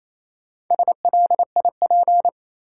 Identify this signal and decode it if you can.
Morse code, heard as a keyed tone, about 28 wpm, 700 Hz. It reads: SLIP
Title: Four Letter Words - 28wpm